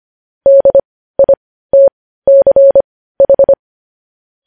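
Morse code, heard as a keyed tone, about 25 wpm, spaced as for 15 wpm, 550 Hz. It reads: DITCH
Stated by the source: Morse code news headlines sent at 15, opening with vvv.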